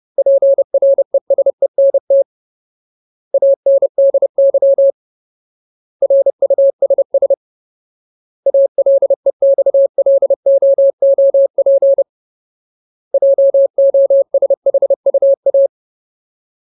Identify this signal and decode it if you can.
Morse code, heard as a keyed tone, about 30 wpm, 550 Hz. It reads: PRESENT ANDY RUSS ALEXLOOP JOSHUA
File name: Ultimate Word List / 30WPM Ultimate